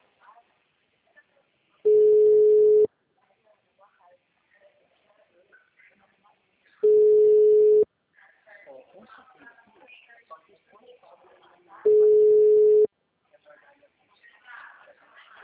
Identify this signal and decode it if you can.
Ringback tone (Europe, 425 Hz), with speech.